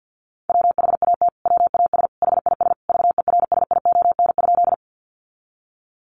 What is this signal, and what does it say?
Morse code, heard as a keyed tone, about 50 wpm, 700 Hz. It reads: WHAT WAS HIS VERSION?
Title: Sentences from Top 1400 Words 50wpm